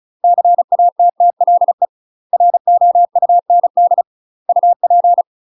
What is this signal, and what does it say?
Morse code, heard as a keyed tone, about 35 wpm, 700 Hz. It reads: CATTLE ROUND UP
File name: UK MASTER QSOs TRAINING/35WPM